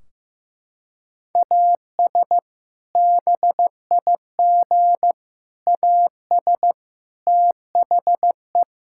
Morse code, heard as a keyed tone, about 16 wpm, 700 Hz. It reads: AS BIG AS THE
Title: Sentences from Top 1600 Words 15wpm